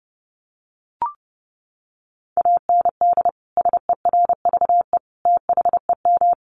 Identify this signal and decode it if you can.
Morse code, heard as a keyed tone, about 30 wpm, 700 Hz. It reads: AND SERVE THEM
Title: Sentences from Top 400 Words 30wpm